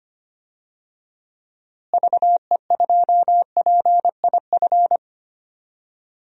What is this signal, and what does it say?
Morse code, heard as a keyed tone, about 25 wpm, 700 Hz. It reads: VE2PIF
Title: DXCC-100 Call Signs - 6 Characters - Repeated 2x - 25wpm